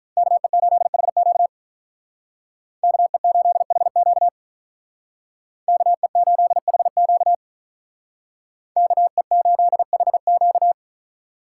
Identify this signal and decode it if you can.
Morse code, heard as a keyed tone, about 43 wpm, 700 Hz. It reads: KE8HQ KE8HQ KE8HQ KE8HQ
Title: Speed Racing - USA Call Signs 35wpm